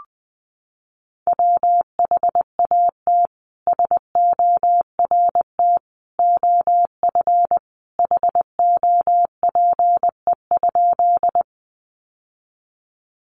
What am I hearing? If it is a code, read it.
Morse code, heard as a keyed tone, about 20 wpm, 700 Hz. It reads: WHAT SORT OF HOPE?
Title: Sentences from Top 1300 Words 20wpm